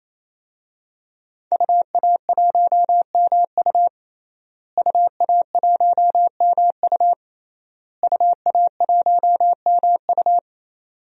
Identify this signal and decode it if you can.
Morse code, heard as a keyed tone, about 28 wpm, 700 Hz. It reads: UA1MU UA1MU UA1MU
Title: DXCC-100 Call Signs - 5 Characters - Repeated 3x - 28wpm